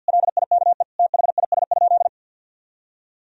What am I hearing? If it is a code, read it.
Morse code, heard as a keyed tone, about 51 wpm, 700 Hz. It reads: LIKE THIS?
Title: Sentences from Top 1000 Words - Review Part 2 - 50wpm